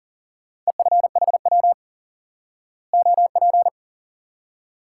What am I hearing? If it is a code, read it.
Morse code, heard as a keyed tone, about 40 wpm, 700 Hz. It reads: EFHW OP
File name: IQE Course 3 - Lesson 89 - YAGI - Two Elements - 40wpm